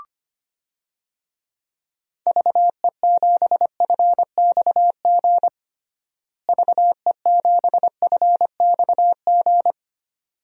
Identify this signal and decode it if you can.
Morse code, heard as a keyed tone, about 25 wpm, 700 Hz. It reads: VE7FXG VE7FXG
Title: DXCC-100 Call Signs - 6 Characters - Repeated 2x - 25wpm